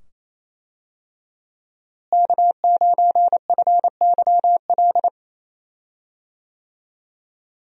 Morse code, heard as a keyed tone, about 28 wpm, 700 Hz. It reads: K9FYL
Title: USA Call Signs - 5 Characters - Repeated 2x - 28wpm